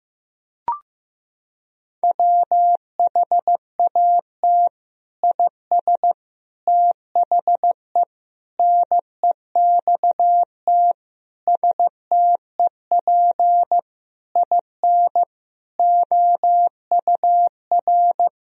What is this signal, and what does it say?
Morse code, heard as a keyed tone, about 15 wpm, 700 Hz. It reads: WHAT IS THE NEXT STEP IN OUR
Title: Sentences from Top 1000 Words - Review Part 4 - 15wpm